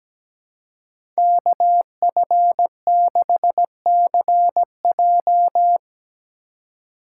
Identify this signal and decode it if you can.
Morse code, heard as a keyed tone, about 17 wpm, 700 Hz. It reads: KF6CJ
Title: USA Call Signs - 5 Characters - 4x Character Spacing - 17wpm